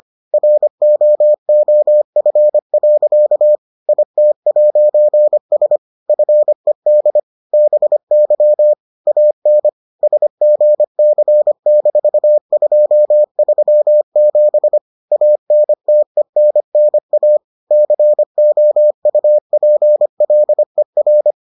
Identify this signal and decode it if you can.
Morse code, heard as a keyed tone, about 25 wpm, 600 Hz. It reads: ROOF. IT'S FED BY AN SGC-237 ANTENNA COUPLER